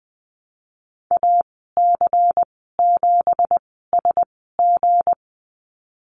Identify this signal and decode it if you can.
Morse code, heard as a keyed tone, about 20 wpm, spaced as for 15 wpm, 700 Hz. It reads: AC7SG
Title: 179 Lesson 29g - Slash - Call Signs 20wpm